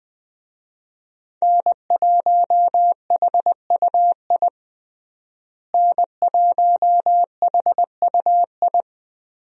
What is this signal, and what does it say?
Morse code, heard as a keyed tone, about 20 wpm, 700 Hz. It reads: N1HUI N1HUI
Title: USA Call Signs - 5 Characters - Repeated 3x - 20wpm